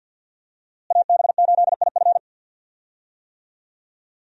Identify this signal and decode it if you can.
Morse code, heard as a keyed tone, about 51 wpm, 700 Hz. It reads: AB8IF
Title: USA Call Signs - 5 Characters - Repeated 2x - 50wpm